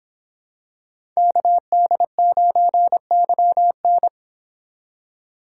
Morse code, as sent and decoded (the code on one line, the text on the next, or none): -.- -.. ----. -.-- -.
KD9YN